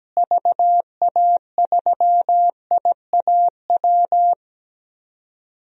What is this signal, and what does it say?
Morse code, heard as a keyed tone, about 17 wpm, 700 Hz. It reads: VA3IAW
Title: DXCC-100 Call Signs - 6 Characters - Repeated 3x - 17wpm